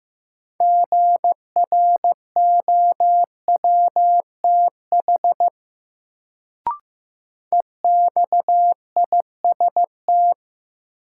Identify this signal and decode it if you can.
Morse code, heard as a keyed tone, about 15 wpm, 700 Hz. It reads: GROWTH EXIST
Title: Sentences from Top 1000 Words - Review Part 5 - New Word Drill - 15wpm